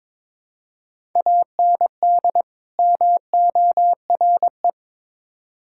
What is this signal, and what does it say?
Morse code, heard as a keyed tone, about 22 wpm, 700 Hz. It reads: AND MORE
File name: Build A Phrase - 2-Word Phrases - 22wpm